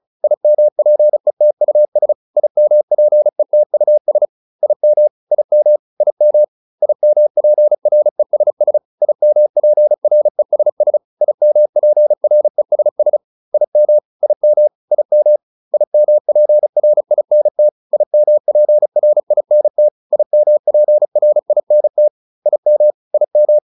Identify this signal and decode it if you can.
Morse code, heard as a keyed tone, about 34 wpm, 600 Hz. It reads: IMPETUS IMPETUS IM IM IM IMPRESS IMPRESS IMPRESS IM IM IM IMPRINT IMPRINT IMPRINT IM IM